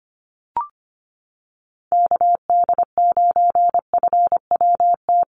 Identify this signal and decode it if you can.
Morse code, heard as a keyed tone, about 25 wpm, 700 Hz. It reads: KD9FWT
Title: USA Call Signs Repeated 2x 25wpm